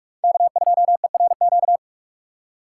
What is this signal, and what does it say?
Morse code, heard as a keyed tone, about 67 wpm, spaced as for 55 wpm, 700 Hz. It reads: K2ERQ
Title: Speed Racing - USA Call Signs 45wpm